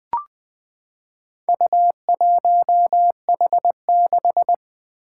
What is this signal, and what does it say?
Morse code, heard as a keyed tone, about 20 wpm, 700 Hz. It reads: U1H6
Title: Four Challenging Letter-Number Combinations 20wpm